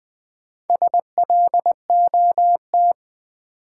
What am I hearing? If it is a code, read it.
Morse code, heard as a keyed tone, about 20 wpm, 700 Hz. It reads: SLOT
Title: Build A Word - 2 Letters At A Time - 4-Letter Words - 20wpm